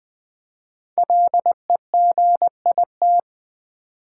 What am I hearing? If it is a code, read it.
Morse code, heard as a keyed tone, about 20 wpm, 700 Hz. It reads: LEGIT
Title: Build A Word - 5 Letters At A Time - 10-Letter Words - 20wpm